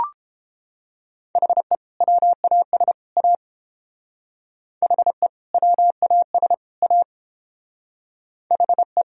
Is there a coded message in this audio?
Yes: Morse code from 1.4 to 9.0 s, heard as a keyed tone, about 29 wpm, 700 Hz. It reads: HE WAS A HE WAS A HE